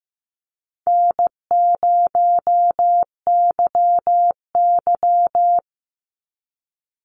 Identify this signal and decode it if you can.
Morse code, heard as a keyed tone, about 15 wpm, 700 Hz. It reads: N0YY
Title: Easy Call Signs Repeated 2x - 15wpm